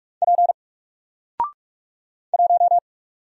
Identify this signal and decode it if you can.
Morse code, heard as a keyed tone, about 45 wpm, 700 Hz. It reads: P 1
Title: Single Letter-Number 45wpm